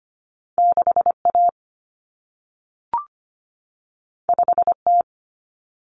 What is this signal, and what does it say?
Morse code, heard as a keyed tone, about 25 wpm, 700 Hz. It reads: 6A 5T
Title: Common Cut Numbers - 2-Digit 25wpm